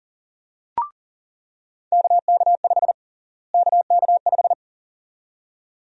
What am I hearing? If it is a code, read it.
Morse code, heard as a keyed tone, about 40 wpm, 700 Hz. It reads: KK5 KK5
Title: USA Call Sign Prefixes - Repeated 2x - 40wpm